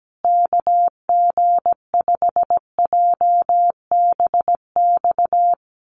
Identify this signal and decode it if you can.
Morse code, heard as a keyed tone, about 17 wpm, 700 Hz. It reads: KG5JBX
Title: USA Call Signs Repeated 3x 17wpm